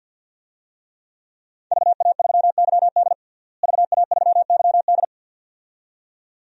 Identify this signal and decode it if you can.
Morse code, heard as a keyed tone, about 51 wpm, 700 Hz. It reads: VA3YD VA3YD
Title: DXCC-100 Call Signs - 5 Characters - Repeated 2x - 50wpm